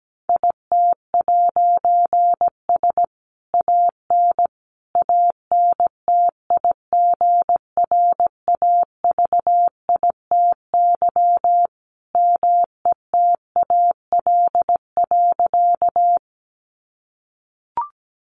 Morse code, heard as a keyed tone, about 17 wpm, 700 Hz. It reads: IT'S AN ANTIGRAVITY METAL.
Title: A Sound Of Thunder 17wpm